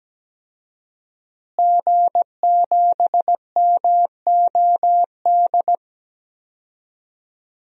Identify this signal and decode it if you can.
Morse code, heard as a keyed tone, about 17 wpm, 700 Hz. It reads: G7MOD
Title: DXCC-100 Call Signs - 5 Characters - Repeated 3x - 17wpm